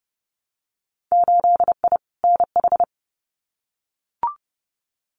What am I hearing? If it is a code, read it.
Morse code, heard as a keyed tone, about 30 wpm, 700 Hz. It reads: 8I NH
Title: Winter Field Day Exchanges Encore 30wpm